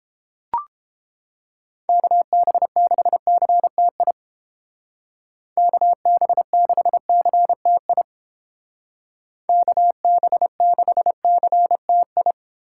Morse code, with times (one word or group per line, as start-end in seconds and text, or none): 1.9-4.1 KB6CTI
5.6-8.0 KB6CTI
9.5-12.3 KB6CTI